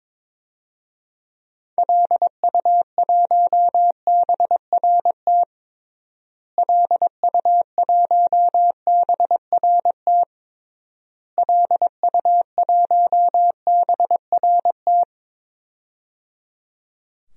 Morse code, heard as a keyed tone, about 22 wpm, 700 Hz. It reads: LU1BRT LU1BRT LU1BRT
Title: DXCC-100 Call Signs - 6 Characters - Repeated 3x - 22wpm